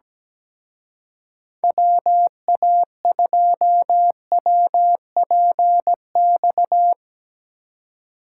Morse code, heard as a keyed tone, about 18 wpm, 700 Hz. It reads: WA2WPX
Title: Speed Racing - USA Call Signs 17wpm